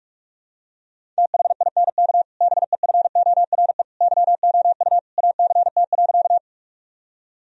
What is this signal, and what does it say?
Morse code, heard as a keyed tone, about 46 wpm, 700 Hz. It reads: THINK BEFORE YOU ACT.
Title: Sentences from Top 1000 Words - Review Part 1 - 45wpm